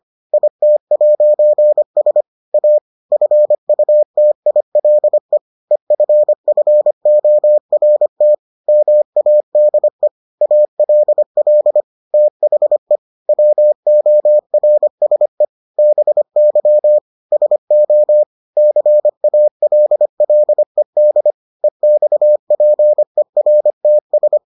Morse code, heard as a keyed tone, about 25 wpm, 600 Hz. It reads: IT'S A FUTILE EFFORT MADE ALL THE WORSE BY SO CALLED EXPERTS